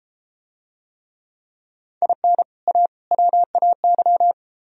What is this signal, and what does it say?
Morse code, heard as a keyed tone, about 29 wpm, 700 Hz. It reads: IN A WAY